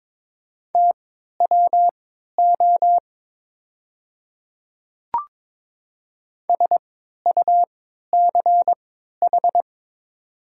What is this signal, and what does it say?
Morse code, heard as a keyed tone, about 22 wpm, spaced as for 12 wpm, 690 Hz. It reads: TWO SUCH
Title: 3x Character Spacing - Top 200 Words - 22wpm